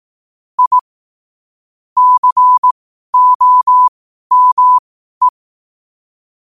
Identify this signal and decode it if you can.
Morse code, heard as a keyed tone, about 18 wpm, spaced as for 12 wpm, 1000 Hz. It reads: I COME
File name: qotdfiles / 12wpm